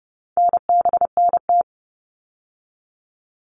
Morse code, heard as a keyed tone, about 30 wpm, 700 Hz. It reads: NBNT
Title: All Cut Numbers - 4-Digit 30wpm